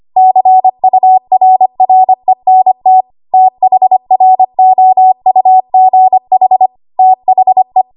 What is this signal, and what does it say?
Morse code, heard as a keyed tone, about 25 wpm, 750 Hz. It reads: CURRENT THROUGH THE
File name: Archive / 25 WPM / 25WPM